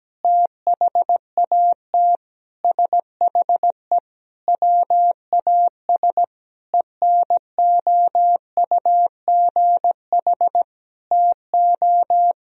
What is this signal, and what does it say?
Morse code, heard as a keyed tone, about 17 wpm, 700 Hz. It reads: THAT SHE WAS ENOUGH TO
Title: Sentences from Top 1000 Words - Review Part 3 - 17wpm